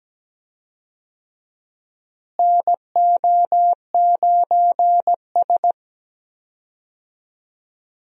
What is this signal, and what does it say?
Morse code, heard as a keyed tone, about 17 wpm, 700 Hz. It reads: NO9S
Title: USA Call Signs - 4 Characters - Repeated 3x - 17wpm